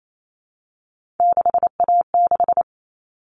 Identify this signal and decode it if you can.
Morse code, heard as a keyed tone, about 28 wpm, 700 Hz. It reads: 6A6
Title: All Cut Numbers - 3-Digit 28wpm